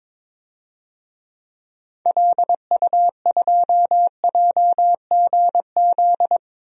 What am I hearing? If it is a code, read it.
Morse code, heard as a keyed tone, about 22 wpm, 700 Hz. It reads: LU2JGZ
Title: DXCC-100 Call Signs - 6 Characters - 22wpm